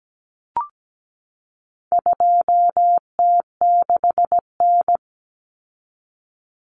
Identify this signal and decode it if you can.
Morse code, heard as a keyed tone, about 17 wpm, 700 Hz. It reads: 2T6N
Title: Common Cut Numbers - 4-Digit 17wpm